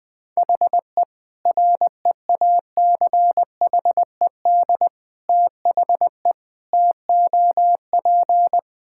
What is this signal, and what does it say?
Morse code, heard as a keyed tone, about 21 wpm, 700 Hz. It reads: HE REACHED THE TOP
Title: Sentences with 3-Word Phrases - 20wpm